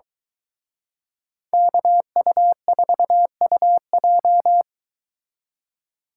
Morse code, heard as a keyed tone, about 23 wpm, 700 Hz. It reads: KU4UJ